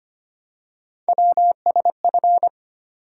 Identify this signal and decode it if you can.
Morse code, heard as a keyed tone, about 25 wpm, 700 Hz. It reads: WSF
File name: Three Letter-Number Combinations - 25wpm